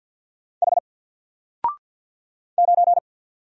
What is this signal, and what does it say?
Morse code, heard as a keyed tone, about 50 wpm, 700 Hz. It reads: H 9
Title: Single Letter-Number 50wpm